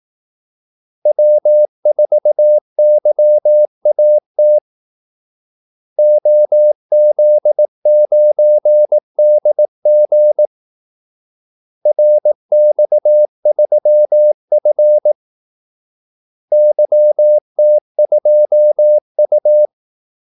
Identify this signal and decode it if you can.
Morse code, heard as a keyed tone, about 18 wpm, 600 Hz. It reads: W4YAT OZ9DG RX3F YT2U
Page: Kaldesignaler 18 wpm